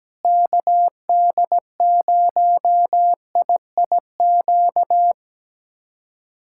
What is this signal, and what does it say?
Morse code, heard as a keyed tone, about 17 wpm, 700 Hz. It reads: KD0IIQ
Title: USA Callsign DE Callsign - 6 Characters - 17wpm